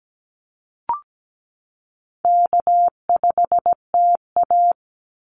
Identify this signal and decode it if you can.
Morse code, heard as a keyed tone, about 17 wpm, 700 Hz. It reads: K5TA
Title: Easy Call Signs - 17wpm